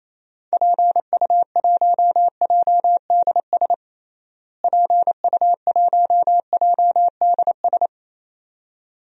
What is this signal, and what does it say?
Morse code, heard as a keyed tone, about 28 wpm, 700 Hz. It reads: PU1JDS PU1JDS
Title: DXCC-100 Call Signs - 6 Characters - Repeated 2x - 28wpm